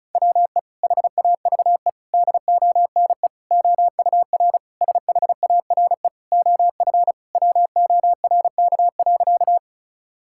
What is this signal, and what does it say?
Morse code, heard as a keyed tone, about 34 wpm, 700 Hz. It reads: WE HAVE DONE OUR SHARE OF WORK.
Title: Sentences from Top 1000 Words - Review Part 6 - 35wpm